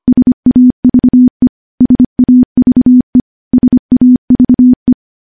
Morse code, heard as a keyed tone, about 25 wpm, 250 Hz. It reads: SAVE SAVE SAVE
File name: morse-training-25WPM